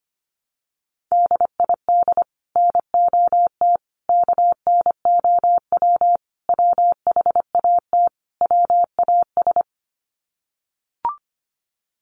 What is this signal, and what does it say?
Morse code, heard as a keyed tone, about 25 wpm, 700 Hz. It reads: DID NOT KNOW WHAT WAS
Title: Sets of 5 Words Encore2 25wpm